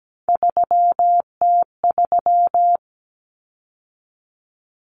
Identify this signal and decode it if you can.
Morse code, heard as a keyed tone, about 17 wpm, 700 Hz. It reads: 3T3
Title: Common Cut Numbers - 3-Digit 17wpm